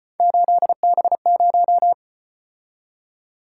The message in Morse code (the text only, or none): 8B0